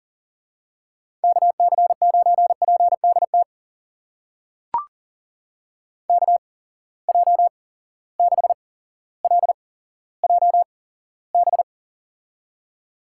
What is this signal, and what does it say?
Morse code, heard as a keyed tone, about 39 wpm, spaced as for 17 wpm, 700 Hz. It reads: KC9PDT KJ6LJB
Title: USA Call Signs - 6 Characters - 8x Character Spacing - 40wpm